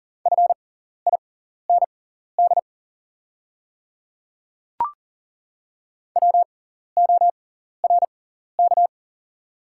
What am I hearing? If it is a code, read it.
Morse code, heard as a keyed tone, about 38 wpm, spaced as for 14 wpm, 700 Hz. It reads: FIND WORK
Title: 6x Character Spacing - Top 500 Words - 40wpm